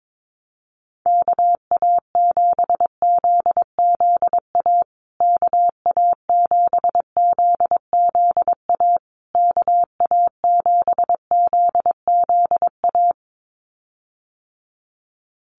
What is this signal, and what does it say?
Morse code, heard as a keyed tone, about 22 wpm, 700 Hz. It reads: KA7ZZA KA7ZZA KA7ZZA
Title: USA Call Signs Repeated 3x 22wpm